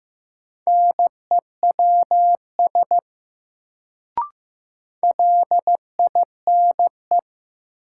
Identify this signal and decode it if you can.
Morse code, heard as a keyed tone, about 15 wpm, 700 Hz. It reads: NEWS LINE
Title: Sets of 5 Words - New Word Drill - 15wpm